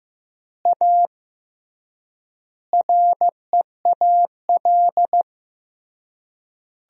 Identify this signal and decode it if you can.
Morse code, heard as a keyed tone, about 16 wpm, 700 Hz. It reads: A REAL
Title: 3x Word-Spacing - Sentences from Top 500 Words - 15wpm